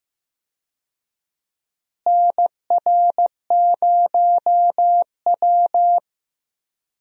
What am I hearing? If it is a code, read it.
Morse code, heard as a keyed tone, about 15 wpm, 700 Hz. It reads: NR0W
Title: USA Call Signs - 4 Characters - Repeated 2x - 15wpm